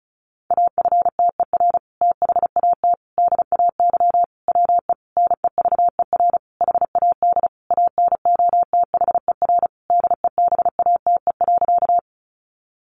Morse code, heard as a keyed tone, about 34 wpm, 700 Hz. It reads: AFTER THAT DAY WE NEVER HAD ANOTHER DEBATE.
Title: Sentences from Top 1100 Words 35wpm